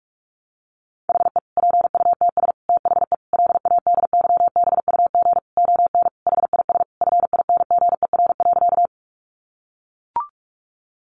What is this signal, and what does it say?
Morse code, heard as a keyed tone, about 44 wpm, 700 Hz. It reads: HE PUTS THE LADYBUG ON HIS FINGER.
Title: Additional Easy Phrases 45wpm